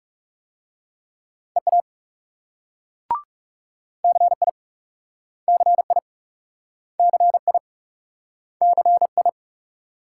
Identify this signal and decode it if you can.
Morse code, heard as a keyed tone, about 37 wpm, 700 Hz. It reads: EA CI CI CI CI
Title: Speed-Racing Multi-letter Phonemes - 30wpm